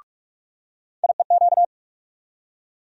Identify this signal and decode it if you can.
Morse code, heard as a keyed tone, about 47 wpm, 700 Hz. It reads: IEQ
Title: USA Call Sign Postfixes - 3 Letters 45wpm